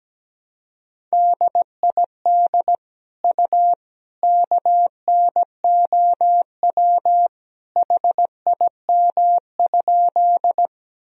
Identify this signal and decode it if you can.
Morse code, heard as a keyed tone, about 17 wpm, 700 Hz. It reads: DID U KNOW HIM?
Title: Sentences from Top 100 Words and Abbreviations 17wpm